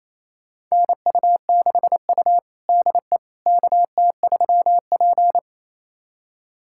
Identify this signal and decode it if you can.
Morse code, heard as a keyed tone, about 28 wpm, 700 Hz. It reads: NU6U DE KT3P
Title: USA Callsign DE Callsign - 4 Characters - 28wpm